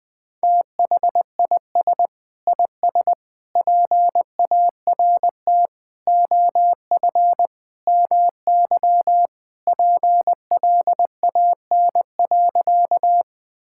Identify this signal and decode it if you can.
Morse code, heard as a keyed tone, about 20 wpm, 700 Hz. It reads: THIS IS PART OF MY PLAN.
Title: Sentences with 2-Word Phrases - 20wpm